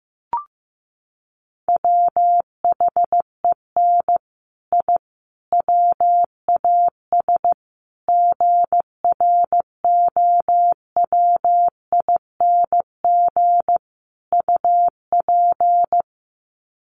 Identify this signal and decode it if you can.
Morse code, heard as a keyed tone, about 15 wpm, 700 Hz. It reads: WHEN I WAS GROWING UP
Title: Sets of 5 Words Encore 15wpm